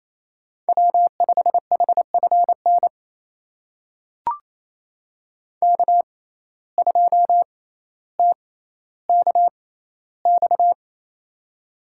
Morse code, heard as a keyed tone, about 28 wpm, spaced as for 14 wpm, 700 Hz. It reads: W5HFN K2TKX
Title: USA Call Signs - 5 Characters - 6x Character Spacing - 28wpm